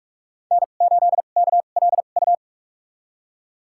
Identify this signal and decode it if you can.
Morse code, heard as a keyed tone, about 44 wpm, spaced as for 32 wpm, 700 Hz. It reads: N8KLU
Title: USA Call Signs - 2x Character Spacing - 45wpm